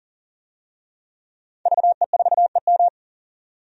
Morse code, heard as a keyed tone, about 40 wpm, 700 Hz. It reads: VE4EM
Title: DXCC-100 Call Signs - 5 Characters - 40wpm